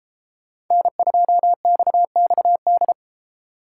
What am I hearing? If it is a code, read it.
Morse code, heard as a keyed tone, about 36 wpm, 700 Hz. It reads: N2XXD